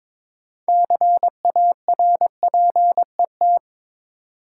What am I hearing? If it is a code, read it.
Morse code, heard as a keyed tone, about 22 wpm, 700 Hz. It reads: CARPET
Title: Six Letter Words - 22wpm